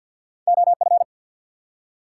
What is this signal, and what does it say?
Morse code, heard as a keyed tone, about 50 wpm, 700 Hz. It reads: OF
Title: Two Letter Words - 50wpm